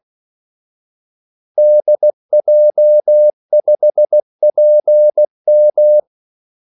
Morse code, heard as a keyed tone, about 16 wpm, 600 Hz. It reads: DJ5PM
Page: Kaldesignaler 16 wpm